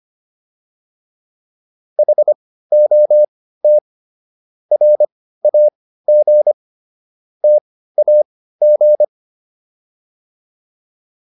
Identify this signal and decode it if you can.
Morse code, heard as a keyed tone, about 25 wpm, spaced as for 15 wpm, 600 Hz. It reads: HOT RAG TAG